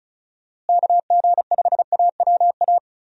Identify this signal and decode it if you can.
Morse code, heard as a keyed tone, about 36 wpm, 700 Hz. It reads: KG5AWA
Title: USA Call Signs - 6 Characters - Repeated 2x - 35wpm